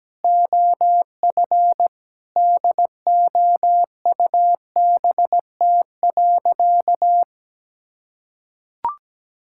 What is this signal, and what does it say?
Morse code, heard as a keyed tone, about 17 wpm, 700 Hz. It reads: OF DOUBT.
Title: Sentences from Top 1600 Words 17wpm